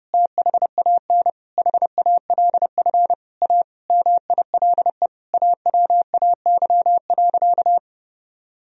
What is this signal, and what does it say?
Morse code, heard as a keyed tone, about 32 wpm, 700 Hz. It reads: THAN HALF A MILE AWAY.
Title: Sentences from Top 1000 Words - Review Part 2 - 30wpm